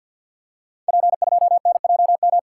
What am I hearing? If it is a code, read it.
Morse code, heard as a keyed tone, about 46 wpm, 690 Hz. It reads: P2NJM
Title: Code Groups - 5 Letter:Numbers - 50wpm